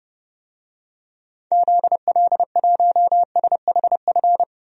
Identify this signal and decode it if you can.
Morse code, heard as a keyed tone, about 30 wpm, 700 Hz. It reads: ZL1SHF
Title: DXCC-100 Call Signs - 6 Characters - 30wpm